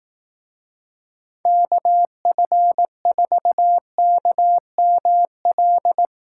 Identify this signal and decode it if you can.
Morse code, heard as a keyed tone, about 18 wpm, 700 Hz. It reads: KF4KML
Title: Speed Racing - USA 2x3 Call Signs 15wpm